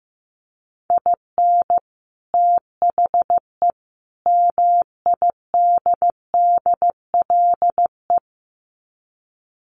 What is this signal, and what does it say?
Morse code, heard as a keyed tone, about 14 wpm, 700 Hz. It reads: IN THE MIDDLE
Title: Sets of 3 Words 15wpm